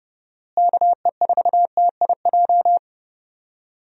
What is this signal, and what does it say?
Morse code, heard as a keyed tone, about 32 wpm, 700 Hz. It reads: KE4TIJ
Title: USA Call Signs - 6 Characters - Repeated 2x - 30wpm